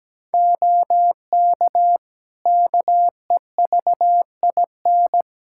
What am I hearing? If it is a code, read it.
Morse code, heard as a keyed tone, about 17 wpm, 700 Hz. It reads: OK KEVIN
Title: Practice QSO - Exchange 3 - More Info - 17wpm